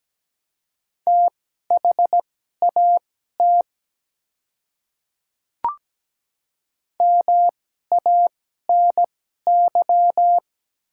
Morse code, heard as a keyed tone, about 17 wpm, spaced as for 12 wpm, 700 Hz. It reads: THAT MANY
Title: 2x Character Spacing - Top 300 Words - 17wpm